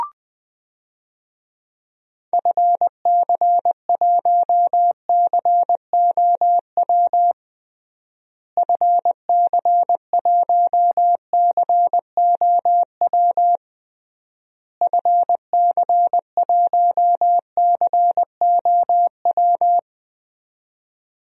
Morse code, heard as a keyed tone, about 20 wpm, 700 Hz. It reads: FC1COW FC1COW FC1COW
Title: DXCC-100 Call Signs - 6 Characters - Repeated 3x - 20wpm